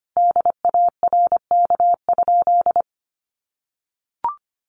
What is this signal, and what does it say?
Morse code, heard as a keyed tone, about 25 wpm, 700 Hz. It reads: DARK?
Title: Sentences from Top 900 Words 25wpm